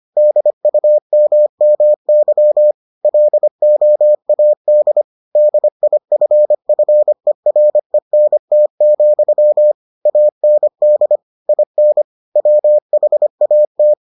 Morse code, heard as a keyed tone, about 25 wpm, 600 Hz. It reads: DUMMY LOAD DIFFERENT, AND IN WHAT